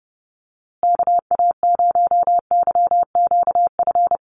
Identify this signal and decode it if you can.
Morse code, heard as a keyed tone, about 30 wpm, 700 Hz. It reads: KA0YQF
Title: USA Call Signs Repeated 2x 30wpm